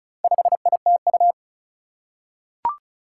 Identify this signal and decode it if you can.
Morse code, heard as a keyed tone, about 36 wpm, 700 Hz. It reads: 5ITU
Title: Four Challenging Letter-Number Combinations 35wpm